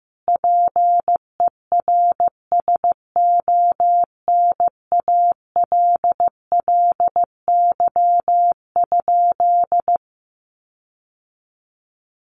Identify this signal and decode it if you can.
Morse code, heard as a keyed tone, about 15 wpm, 700 Hz. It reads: PERSONALLY?
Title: Sentences from Top 1400 Words 15wpm